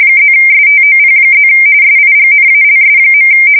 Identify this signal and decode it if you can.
RTTY, about 45.45 baud. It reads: +STOP+ SIGNALS.
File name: fsk170 / morkrum